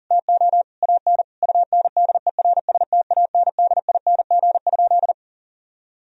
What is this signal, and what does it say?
Morse code, heard as a keyed tone, about 40 wpm, 700 Hz. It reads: TO AN UNDERSTANDING?
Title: Sentences from Top 1000 Words - Review Part 3 - 40wpm